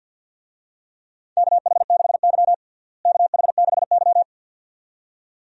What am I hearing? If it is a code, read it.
Morse code, heard as a keyed tone, about 48 wpm, 700 Hz. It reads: KH6Y KH6Y
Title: DXCC-100 Call Signs - 4 Characters - Repeated 2x - 50wpm